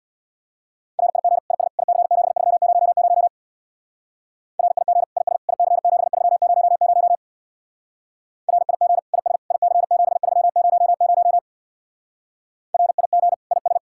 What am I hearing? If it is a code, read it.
Morse code, heard as a keyed tone, about 61 wpm, 700 Hz. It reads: RIG IS IC7300 RIG IS IC7300 RIG IS IC7300 RIG IS
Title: Speed-Racing Common QSO Phrases - 50wpm